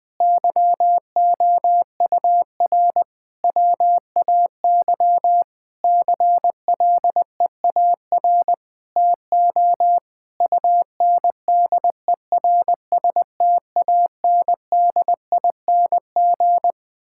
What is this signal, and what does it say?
Morse code, heard as a keyed tone, about 20 wpm, 700 Hz. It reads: YOUR WAY CLEAR TO UNDERSTANDING
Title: Sentences with 9-Word Phrases - 20wpm